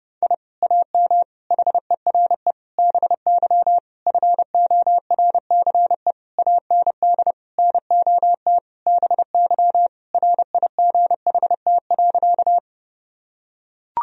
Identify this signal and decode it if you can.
Morse code, heard as a keyed tone, about 30 wpm, 700 Hz. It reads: I AM HERE BY FORCE AND NOT BY RIGHT.
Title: Sentences from Top 1000 Words - Review Part 3 - 30wpm